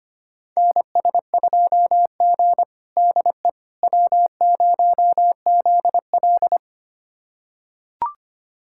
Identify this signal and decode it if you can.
Morse code, heard as a keyed tone, about 25 wpm, 700 Hz. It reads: NS2G DE W0ZL
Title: USA Callsign DE Callsign - 4 Characters - 25wpm